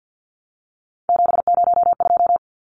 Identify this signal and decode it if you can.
Morse code, heard as a keyed tone, about 51 wpm, 700 Hz. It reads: 702
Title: Numbers - 3-Digit 50wpm